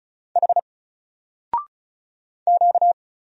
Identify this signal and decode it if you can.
Morse code, heard as a keyed tone, about 34 wpm, 700 Hz. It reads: H Q
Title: Single Character (Letter:Number:Punctuation:Prosign) 35wpm